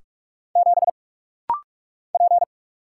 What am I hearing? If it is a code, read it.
Morse code, heard as a keyed tone, about 46 wpm, 700 Hz. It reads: 7 P
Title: Single Letter-Number - Rapid-Fire 45wpm